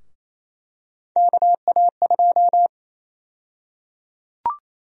Morse code, heard as a keyed tone, about 28 wpm, 700 Hz. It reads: KA2
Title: USA Call Sign Prefixes - 28wpm